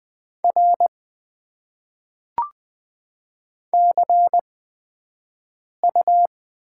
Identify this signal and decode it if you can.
Morse code, heard as a keyed tone, about 20 wpm, spaced as for 5 wpm, 700 Hz. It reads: R CU
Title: Top 90 QSO Elements - 8x Character Spacing - 20wpm